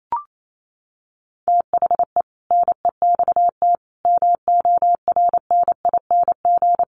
Morse code, heard as a keyed tone, about 28 wpm, 700 Hz. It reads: THE NEXT MORNING
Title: Sentences from Top 1200 Words 28wpm